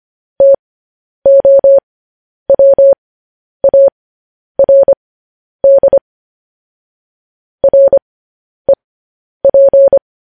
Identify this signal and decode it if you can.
Morse code, heard as a keyed tone, about 25 wpm, spaced as for 10 wpm, 550 Hz. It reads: TOWARD REP
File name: podcast / News headlines-1x-25x10